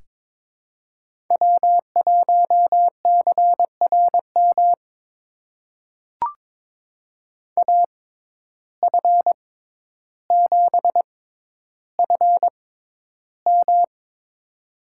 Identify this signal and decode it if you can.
Morse code, heard as a keyed tone, about 22 wpm, spaced as for 11 wpm, 700 Hz. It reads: W1CRM AF7FM
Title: USA Call Signs - 5 Characters - 6x Character Spacing - 22wpm